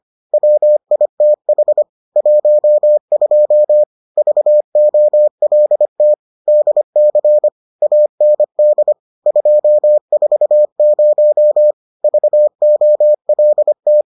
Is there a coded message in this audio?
Yes: Morse code from 0.3 to 14.0 s, heard as a keyed tone, about 25 wpm, 600 Hz. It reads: WITH 12 VOLT DC AND 240 VOLT